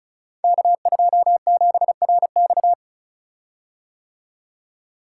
Morse code, heard as a keyed tone, about 36 wpm, 700 Hz. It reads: K27RX
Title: Code Groups - 5 Letter:Numbers - 35wpm